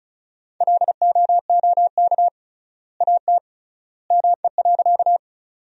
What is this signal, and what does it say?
Morse code, heard as a keyed tone, about 36 wpm, 700 Hz. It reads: LOOK AT ME.
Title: 3x Word-Spacing - Sentences from Top 200 Words - 35wpm